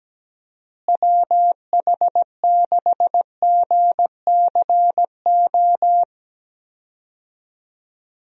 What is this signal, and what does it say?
Morse code, heard as a keyed tone, about 17 wpm, 700 Hz. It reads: WH6GCO
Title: USA Call Signs - 6 Characters - Repeated 3x - 17wpm